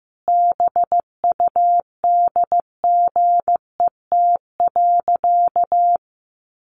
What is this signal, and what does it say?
Morse code, heard as a keyed tone, about 15 wpm, 700 Hz. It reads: BUDGET.
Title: Sentences from Top 1200 Words 15wpm